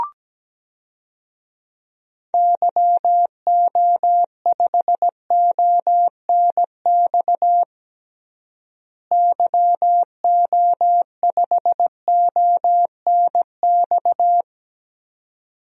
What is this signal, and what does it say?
Morse code, heard as a keyed tone, about 17 wpm, 700 Hz. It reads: YO5ONX YO5ONX
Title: DXCC-100 Call Signs - 6 Characters - Repeated 2x - 17wpm